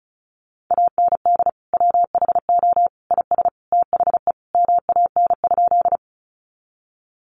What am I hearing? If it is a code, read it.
Morse code, heard as a keyed tone, about 34 wpm, 700 Hz. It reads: AND WHO IS THE MAN?
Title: Sentences from Top 200 Words 35wpm